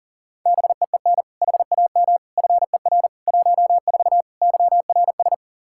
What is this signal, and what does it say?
Morse code, heard as a keyed tone, about 40 wpm, 700 Hz. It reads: BEEN HAM FER 14 YRS
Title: Practice QSO - Exchange 3 - More Info - 40wpm